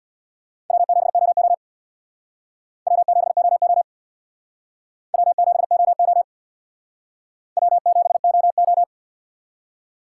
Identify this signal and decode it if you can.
Morse code, heard as a keyed tone, about 60 wpm, 700 Hz. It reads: W7OO W7OO W7OO W7OO
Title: Speed Racing - USA 1x2 Call Signs 50wpm